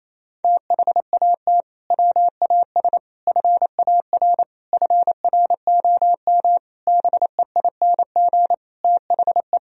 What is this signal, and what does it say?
Morse code, heard as a keyed tone, about 28 wpm, 700 Hz. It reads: THAT WAS FAR FROM BEING THE
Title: Sentences from Top 1000 Words - Review Part 3 - 28wpm